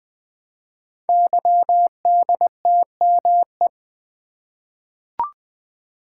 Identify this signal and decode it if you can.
Morse code, heard as a keyed tone, about 20 wpm, 700 Hz. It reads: YDTME
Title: Code Groups - 5 Letters - 20wpm